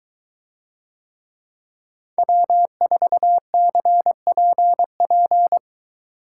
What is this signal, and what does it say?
Morse code, heard as a keyed tone, about 23 wpm, 700 Hz. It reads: W4CPP